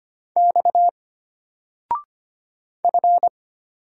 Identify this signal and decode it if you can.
Morse code, heard as a keyed tone, about 25 wpm, 700 Hz. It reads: X F
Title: Single Letter-Number 25wpm